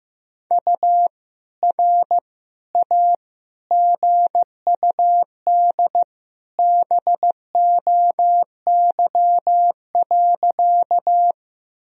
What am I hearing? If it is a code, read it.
Morse code, heard as a keyed tone, about 15 wpm, 700 Hz. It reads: U R A GUD BOY.
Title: Sentences from Top 200 Words and Abbreviations 15wpm